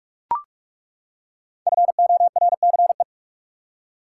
Morse code, heard as a keyed tone, about 42 wpm, 700 Hz. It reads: FORCE
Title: Five Letter Words - 45wpm